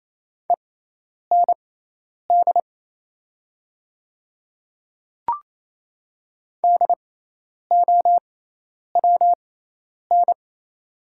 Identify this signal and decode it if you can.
Morse code, heard as a keyed tone, about 28 wpm, spaced as for 10 wpm, 700 Hz. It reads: END DOWN
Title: 6x Character Spacing - Top 300 Words - 28wpm